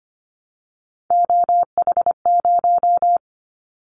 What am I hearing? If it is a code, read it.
Morse code, heard as a keyed tone, about 25 wpm, 700 Hz. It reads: OH0
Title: Callsign Prefixes for Top 100 DXCC - 25wpm